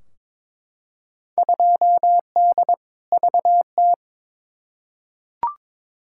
Morse code, heard as a keyed tone, about 22 wpm, 700 Hz. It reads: 2D VT
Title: Field Day Exchanges Encore - 22wpm